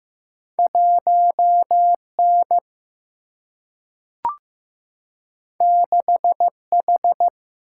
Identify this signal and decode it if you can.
Morse code, heard as a keyed tone, about 15 wpm, 700 Hz. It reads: 1 N 6 H
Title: Single-Letter Single-Number 15wpm